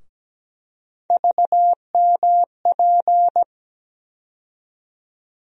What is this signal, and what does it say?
Morse code, heard as a keyed tone, about 17 wpm, 700 Hz. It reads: VMP
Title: USA Call Sign Postfixes - 3 Letters 17wpm